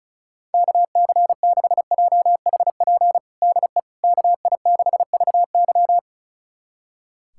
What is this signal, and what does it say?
Morse code, heard as a keyed tone, about 36 wpm, 690 Hz. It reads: KC6JHP DE KI6VY
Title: USA Callsign DE Callsign 35wpm